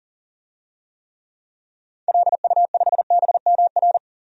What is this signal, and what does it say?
Morse code, heard as a keyed tone, about 40 wpm, 700 Hz. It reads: LU5BMR
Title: DXCC-100 Call Signs - 6 Characters - 40wpm